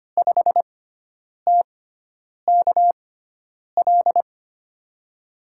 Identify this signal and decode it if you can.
Morse code, heard as a keyed tone, about 25 wpm, spaced as for 9 wpm, 700 Hz. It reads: 5TKL
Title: USA Call Signs - 6 Characters - 6x Character Spacing - 25wpm